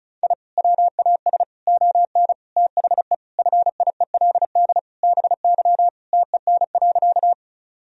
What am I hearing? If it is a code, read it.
Morse code, heard as a keyed tone, about 36 wpm, 700 Hz. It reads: I WAS ON THE FIELD BY TEN.
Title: Sentences from Top 1000 Words - Review Part 2 - 35wpm